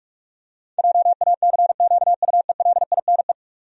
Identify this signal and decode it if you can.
Morse code, heard as a keyed tone, about 42 wpm, 700 Hz. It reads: JACQUELINE
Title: USA Common Names - 45wpm